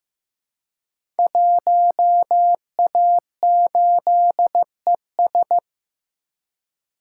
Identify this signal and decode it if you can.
Morse code, heard as a keyed tone, about 15 wpm, 700 Hz. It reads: 1A8ES
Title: Code Groups - 5 Letter:Numbers - 15wpm